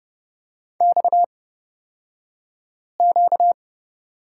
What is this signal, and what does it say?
Morse code, heard as a keyed tone, about 30 wpm, spaced as for 5 wpm, 700 Hz. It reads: XQ
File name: course / 30:10wpm / 235 Lesson 38d - Letter q - Two Characters 30wpm